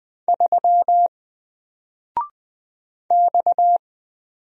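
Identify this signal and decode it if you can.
Morse code, heard as a keyed tone, about 20 wpm, 700 Hz. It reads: 3 X
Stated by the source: Single letter and number Morse drill at 20 wpm, from N4LCD.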